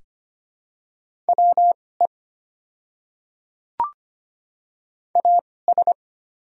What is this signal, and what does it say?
Morse code, heard as a keyed tone, about 25 wpm, spaced as for 18 wpm, 700 Hz. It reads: WE AS
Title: 2x Character Spacing - Top 100 Words - 25wpm